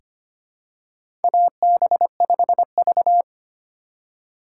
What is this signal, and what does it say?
Morse code, heard as a keyed tone, about 25 wpm, 700 Hz. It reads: AB5V
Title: USA Call Signs - 4 Characters - 2x Character Spacing - 25wpm